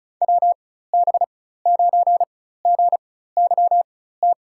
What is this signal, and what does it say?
Morse code, heard as a keyed tone, about 36 wpm, spaced as for 16 wpm, 700 Hz. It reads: WB9GYT
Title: USA Call Signs - 6 Characters - 4x Character Spacing - 35wpm